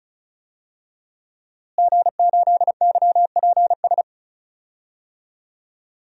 Morse code, heard as a keyed tone, about 36 wpm, 700 Hz. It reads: G8YPS
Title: DXCC-100 Call Signs - 5 Characters - Repeated 2x - 35wpm